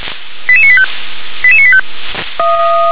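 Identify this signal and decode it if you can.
Five-tone selective call (ZVEI), heard as quick runs of five tones, tones of about 70 ms. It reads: 88985 88985
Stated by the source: ZVEI five-tone selective call received with an SDR.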